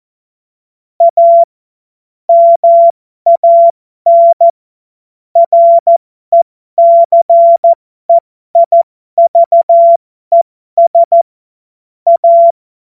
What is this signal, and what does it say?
Morse code, heard as a keyed tone, about 14 wpm, spaced as for 12 wpm, 680 Hz. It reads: A MAN RECEIVES A
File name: Morsekurs / Witze